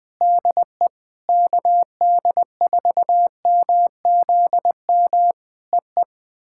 Morse code, heard as a keyed tone, about 20 wpm, 700 Hz. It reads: DE KD4MZM EE
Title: POTA QSO Practice - 20wpm